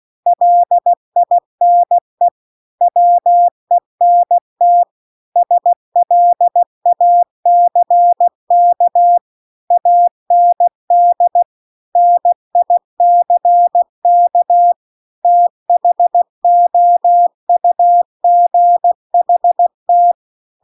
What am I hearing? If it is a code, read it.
Morse code, heard as a keyed tone, about 16 wpm, 700 Hz. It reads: LINE WENT SLACK AND NICK THOUGHT